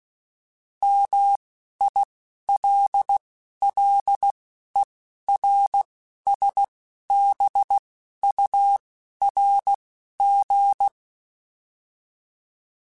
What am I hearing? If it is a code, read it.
Morse code, heard as a keyed tone, about 16 wpm, spaced as for 11 wpm, 800 Hz. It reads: MILLERSBURG
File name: fromSMs / FromSMs mp3s / Cities@10